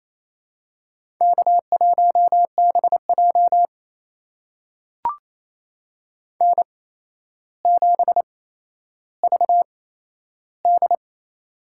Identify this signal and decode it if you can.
Morse code, heard as a keyed tone, about 27 wpm, spaced as for 12 wpm, 700 Hz. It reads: K1BJ N7VD
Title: USA Call Signs - 4 Characters - 8x Character Spacing - 28wpm